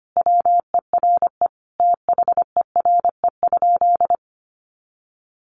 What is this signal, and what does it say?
Morse code, heard as a keyed tone, about 25 wpm, 700 Hz. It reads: WERE THERE?
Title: Sentences from Top 500 Words 25wpm